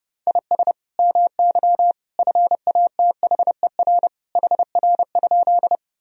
Morse code, heard as a keyed tone, about 30 wpm, 700 Hz. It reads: IS MY FATHER HR?
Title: Sentences from Top 300 Words and Abbreviations 30wpm